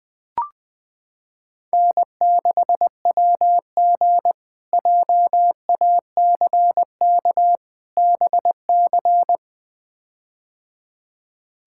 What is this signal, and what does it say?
Morse code, heard as a keyed tone, about 20 wpm, 700 Hz. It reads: N6WG JACK BC
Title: K1USN SST QSO Practice - 20wpm